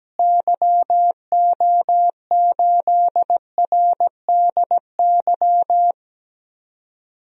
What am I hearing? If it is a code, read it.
Morse code, heard as a keyed tone, about 17 wpm, 700 Hz. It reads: YO8RDY
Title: DXCC-100 Call Signs - 6 Characters - Repeated 3x - 17wpm